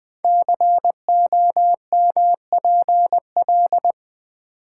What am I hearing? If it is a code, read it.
Morse code, heard as a keyed tone, about 20 wpm, 700 Hz. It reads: COMPL
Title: Build A Word - 5 Letters At A Time - 10-Letter Words - 20wpm